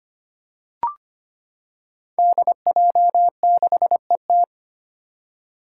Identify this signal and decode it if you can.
Morse code, heard as a keyed tone, about 25 wpm, 700 Hz. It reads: DJ6ET
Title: Five Challenging Letter-Number Combinations 25wpm